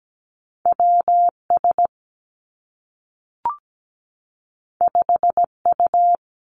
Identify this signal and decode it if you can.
Morse code, heard as a keyed tone, about 17 wpm, 700 Hz. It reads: W S 5 U
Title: Two Challenging Letter-Number Combinations 17wpm